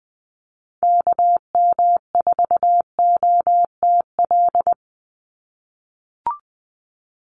Morse code, heard as a keyed tone, about 20 wpm, 700 Hz. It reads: KM4OTL
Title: USA Call Signs 20wpm